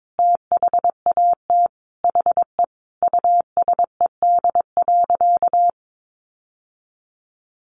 Morse code, heard as a keyed tone, about 22 wpm, 700 Hz. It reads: THAT HE USED.
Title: Sentences from Top 1500 Words 22wpm